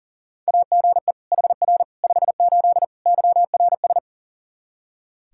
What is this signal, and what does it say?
Morse code, heard as a keyed tone, about 40 wpm, 700 Hz. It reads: AGE HR 58 YRS
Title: Practice QSO - Exchange 3 - More Info - 40wpm